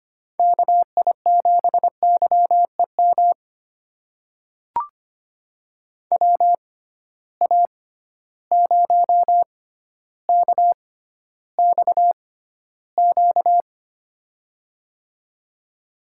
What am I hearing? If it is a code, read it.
Morse code, heard as a keyed tone, about 25 wpm, spaced as for 13 wpm, 700 Hz. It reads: KI7YEM WA0KXQ
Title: USA Call Signs - 6 Characters - 6x Character Spacing - 25wpm